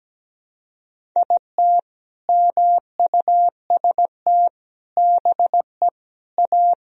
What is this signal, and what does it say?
Morse code, heard as a keyed tone, about 17 wpm, 700 Hz. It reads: IT MUST BE A
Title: Sentences from Top 500 Words and Abbreviations 17wpm